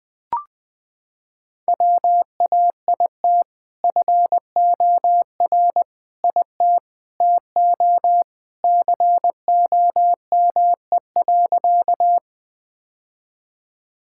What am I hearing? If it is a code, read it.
Morse code, heard as a keyed tone, about 21 wpm, 700 Hz. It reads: WAIT FOR IT TO COME.
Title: Sentences with 2-Word Phrases - 20wpm